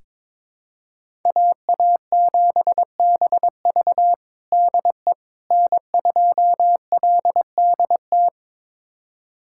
Morse code, heard as a keyed tone, about 22 wpm, 700 Hz. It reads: AA7BV DE N2LDT
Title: USA Callsign DE Callsign - 5 Characters - 22wpm